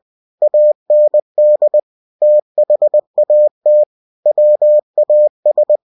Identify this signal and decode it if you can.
Morse code, heard as a keyed tone, about 20 wpm, 600 Hz. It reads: AND THAT WAS
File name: cw-books / winnie-the-pooh / winnie-20wpm-page-0039